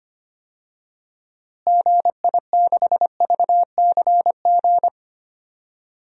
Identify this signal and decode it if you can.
Morse code, heard as a keyed tone, about 25 wpm, 700 Hz. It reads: GI6VCG
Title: DXCC-100 Call Signs - 6 Characters - Repeated 2x - 25wpm